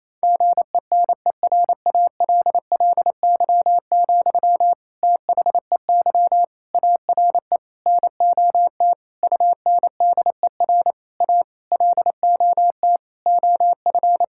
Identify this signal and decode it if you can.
Morse code, heard as a keyed tone, about 27 wpm, 700 Hz. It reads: GENERALLY, THEY ARE NOT UNDER A LOT OF
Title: Sentences from Top 1000 Words - Review Part 6 - 28wpm